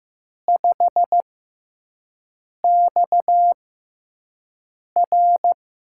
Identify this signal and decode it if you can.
Morse code, heard as a keyed tone, about 15 wpm, spaced as for 5 wpm, 700 Hz. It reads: 5XR
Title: USA Call Signs - 4 Characters - 6x Character Spacing - 15wpm